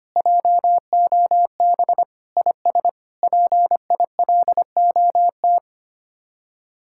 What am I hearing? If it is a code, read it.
Morse code, heard as a keyed tone, about 32 wpm, 700 Hz. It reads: JOB IS PILOT